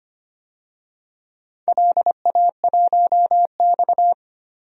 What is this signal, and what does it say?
Morse code, heard as a keyed tone, about 25 wpm, 700 Hz. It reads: LA1X
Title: DXCC-100 Call Signs - 4 Characters - 25wpm